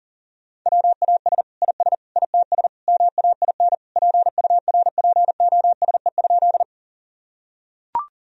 Morse code, heard as a keyed tone, about 40 wpm, 700 Hz. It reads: WAS IS ITS MAIN PURPOSE?
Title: Sentences from Top 1000 Words - Review Part 2 - 40wpm